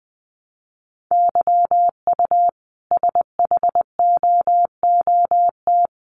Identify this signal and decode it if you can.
Morse code, heard as a keyed tone, about 20 wpm, 700 Hz. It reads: YU SHOOT
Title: A Sound Of Thunder 20wpm